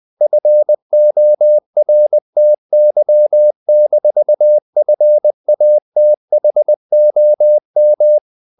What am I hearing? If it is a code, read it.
Morse code, heard as a keyed tone, about 20 wpm, 600 Hz. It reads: FORTY-FATHOM
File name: the old man and the sea / the old man and the sea part 1 20wpm